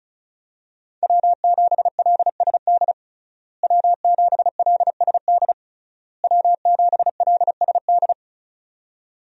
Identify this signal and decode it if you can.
Morse code, heard as a keyed tone, about 34 wpm, 700 Hz. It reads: W7LSD W7LSD W7LSD
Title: USA Call Signs - 5 Characters - Repeated 3x - 35wpm